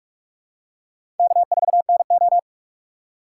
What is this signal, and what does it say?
Morse code, heard as a keyed tone, about 40 wpm, 700 Hz. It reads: K4NO